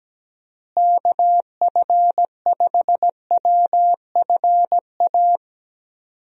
Morse code, heard as a keyed tone, about 17 wpm, 700 Hz. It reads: KF5WFA
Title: USA Call Signs - 6 Characters - Repeated 2x - 17wpm